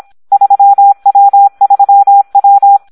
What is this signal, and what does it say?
Morse code, heard as a keyed tone, about 26 wpm, 800 Hz. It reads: 3W3W
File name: Audio practica CW / SD 1 26